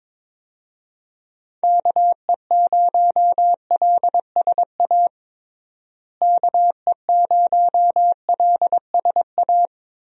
Morse code, heard as a keyed tone, about 22 wpm, 700 Hz. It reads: KE0LSA KE0LSA
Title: USA Call Signs - 6 Characters - Repeated 2x - 22wpm